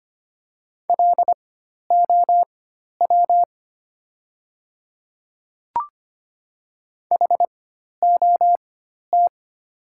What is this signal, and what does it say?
Morse code, heard as a keyed tone, about 25 wpm, spaced as for 12 wpm, 690 Hz. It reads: LOW HOT
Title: 4x Character Spacing - Top 500 Words - 25wpm